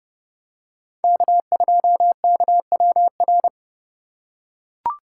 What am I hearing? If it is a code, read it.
Morse code, heard as a keyed tone, about 31 wpm, 700 Hz. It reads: K2KWR